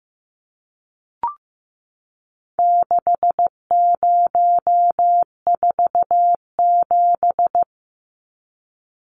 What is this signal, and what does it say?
Morse code, heard as a keyed tone, about 15 wpm, 700 Hz. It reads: 6047
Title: Numbers - 4-Digit 15wpm